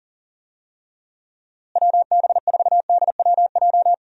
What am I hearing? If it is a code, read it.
Morse code, heard as a keyed tone, about 40 wpm, 700 Hz. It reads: WB4DWJ
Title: USA Call Signs - 6 Characters - Repeated 2x - 40wpm